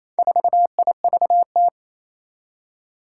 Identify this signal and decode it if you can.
Morse code, heard as a keyed tone, about 27 wpm, 700 Hz. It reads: 4IVT
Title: Four Challenging Letter-Number Combinations 28wpm